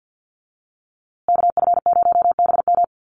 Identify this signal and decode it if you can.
Morse code, heard as a keyed tone, about 50 wpm, 700 Hz. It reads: KF0BM
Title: USA Call Signs Repeated 2x 50wpm